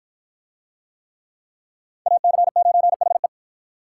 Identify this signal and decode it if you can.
Morse code, heard as a keyed tone, about 48 wpm, 700 Hz. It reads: AC9HE